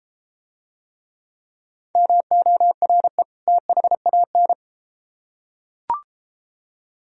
Morse code, heard as a keyed tone, about 33 wpm, 700 Hz. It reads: MORE THAN